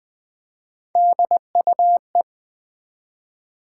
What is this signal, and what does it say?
Morse code, heard as a keyed tone, about 20 wpm, 700 Hz. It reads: DUE
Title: Three Letter Words - 20wpm